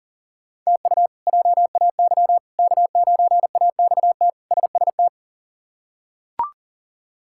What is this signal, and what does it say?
Morse code, heard as a keyed tone, about 40 wpm, 700 Hz. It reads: TU JAY K9AXT SST
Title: K1USN SST QSO Practice - 40wpm